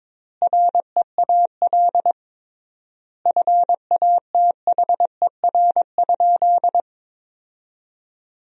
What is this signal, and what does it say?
Morse code, heard as a keyed tone, about 22 wpm, 700 Hz. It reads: REAL FATHER?
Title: 3x Word-Spacing - Sentences from Top 300 Words - 22wpm